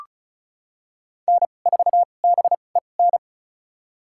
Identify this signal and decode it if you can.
Morse code, heard as a keyed tone, about 36 wpm, spaced as for 25 wpm, 700 Hz. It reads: N4BEN
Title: USA Call Signs - 5 Characters - 2x Character Spacing - 35wpm